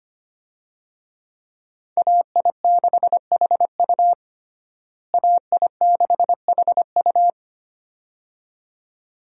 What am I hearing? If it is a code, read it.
Morse code, heard as a keyed tone, about 25 wpm, 700 Hz. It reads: AI6HU AI6HU
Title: USA Call Signs - 5 Characters - Repeated 2x - 25wpm